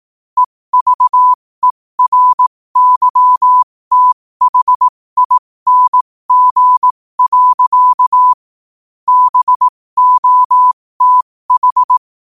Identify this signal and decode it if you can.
Morse code, heard as a keyed tone, about 18 wpm, spaced as for 15 wpm, 1000 Hz. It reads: EVERYTHING. BOTH
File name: qotdfiles / 15wpm